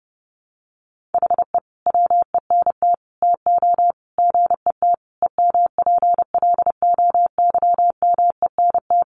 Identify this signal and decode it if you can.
Morse code, heard as a keyed tone, about 30 wpm, 700 Hz. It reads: HE WENT TO GET EMPLOYMENT
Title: Sentences from Top 1500 Words 30wpm